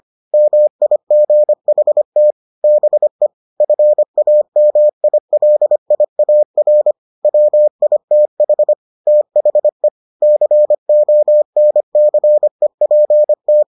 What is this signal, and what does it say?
Morse code, heard as a keyed tone, about 25 wpm, 600 Hz. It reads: MIGHT BE FAMILIAR WITH THE CONCEPT